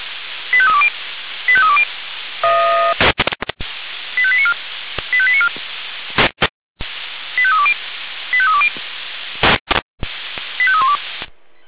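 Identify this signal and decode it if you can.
Five-tone selective call (ZVEI), heard as quick runs of five tones, tones of about 70 ms. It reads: 85320 85320 85894 85894 85320 85320 85312
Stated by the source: ZVEI five-tone selective call 85320 received on SDR.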